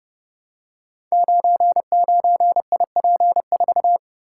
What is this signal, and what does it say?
Morse code, heard as a keyed tone, about 28 wpm, 700 Hz. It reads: 99IP4
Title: Code Groups - 5 Letter:Numbers - 30wpm